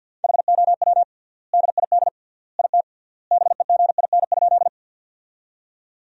Morse code, heard as a keyed tone, about 48 wpm, 700 Hz. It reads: HOW DID IT BEGIN?
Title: 3x Word-Spacing - Sentences from Top 300 Words - 50wpm